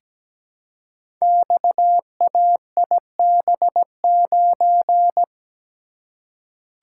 Morse code, heard as a keyed tone, about 17 wpm, 700 Hz. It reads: XAIB9
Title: Code Groups - 5 Letter:Numbers - 17wpm